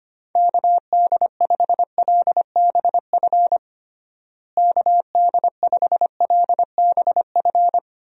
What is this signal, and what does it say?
Morse code, heard as a keyed tone, about 25 wpm, 700 Hz. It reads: KD5LBF KD5LBF
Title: USA Call Signs - 6 Characters - Repeated 3x - 25wpm